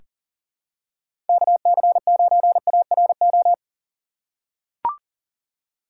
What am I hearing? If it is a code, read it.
Morse code, heard as a keyed tone, about 40 wpm, 700 Hz. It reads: KC9ARO
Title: USA Call Signs - 4x Character Spacing - 40wpm